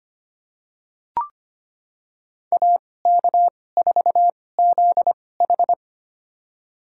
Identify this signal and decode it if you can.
Morse code, heard as a keyed tone, about 25 wpm, spaced as for 18 wpm, 700 Hz. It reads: AK4ZH
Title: USA Call Signs - 5 Characters - 2x Character Spacing - 25wpm